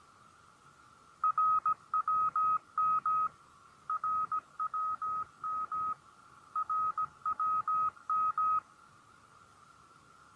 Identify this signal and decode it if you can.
Morse code, heard as a keyed tone, about 16 wpm, 1260 Hz. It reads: RWM RWM RWM